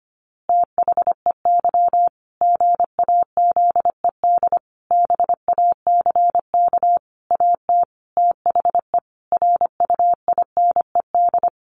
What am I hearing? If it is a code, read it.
Morse code, heard as a keyed tone, about 25 wpm, 700 Hz. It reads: THEY GAZED BACK AT THE RUINED
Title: A Sound Of Thunder 25wpm